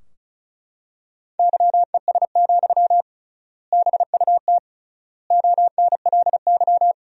Morse code, heard as a keyed tone, about 36 wpm, 700 Hz. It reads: YES, BUT ONLY
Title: 3x Word-Spacing - Sentences from Top 500 Words - 35wpm